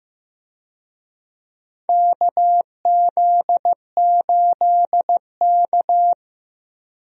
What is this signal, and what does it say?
Morse code, heard as a keyed tone, about 15 wpm, 700 Hz. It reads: KZ8K
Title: USA Call Signs - 4 Characters - Repeated 3x - 15wpm